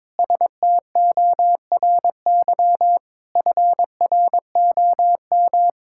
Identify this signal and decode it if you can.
Morse code, heard as a keyed tone, about 22 wpm, 700 Hz. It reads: STORY FROM
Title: Sentences from Top 1000 Words - Review Part 5 - 22wpm